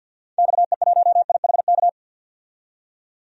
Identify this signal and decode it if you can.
Morse code, heard as a keyed tone, about 52 wpm, 700 Hz. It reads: XE1IHK
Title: DXCC-100 Call Signs - 6 Characters - 50wpm